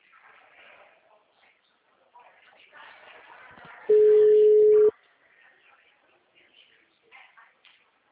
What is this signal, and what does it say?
Ringback tone (Europe, 425 Hz), with speech.